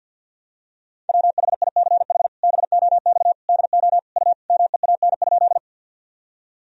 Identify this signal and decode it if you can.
Morse code, heard as a keyed tone, about 51 wpm, 700 Hz. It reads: WHICH BOX DO U MEAN?
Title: Sentences from Top 500 Words and Abbreviations 50wpm